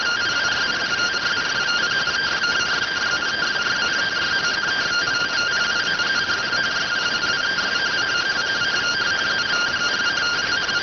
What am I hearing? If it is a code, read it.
Two-tone teleprinter signal, about 170 Hz shift.